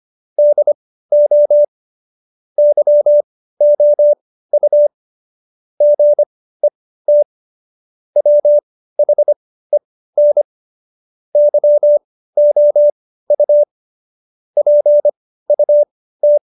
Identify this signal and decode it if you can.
Morse code, heard as a keyed tone, about 25 wpm, spaced as for 15 wpm, 590 Hz. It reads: DO YOU GET WHEN YOU PUT
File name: Jokes-25-char-15-eff